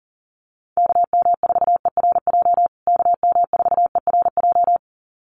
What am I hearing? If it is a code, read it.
Morse code, heard as a keyed tone, about 40 wpm, 700 Hz. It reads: KM4ERJ KM4ERJ
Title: USA Call Signs Repeated 2x 40wpm